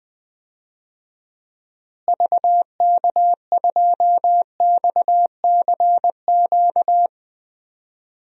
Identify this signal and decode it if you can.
Morse code, heard as a keyed tone, about 21 wpm, 700 Hz. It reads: VK2XCQ
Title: DXCC-100 Call Signs - 6 Characters - Repeated 3x - 20wpm